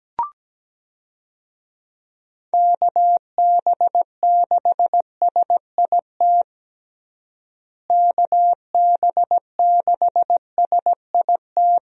Morse code, heard as a keyed tone, about 17 wpm, 700 Hz. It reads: KB6SIT KB6SIT
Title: USA Call Signs - 6 Characters - Repeated 3x - 17wpm